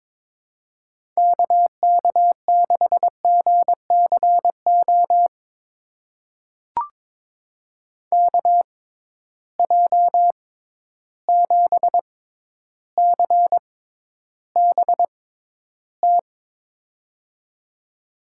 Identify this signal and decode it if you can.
Morse code, heard as a keyed tone, about 22 wpm, spaced as for 11 wpm, 700 Hz. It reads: KK6GCO KJ7CBT
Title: USA Call Signs - 6x Character Spacing - 22wpm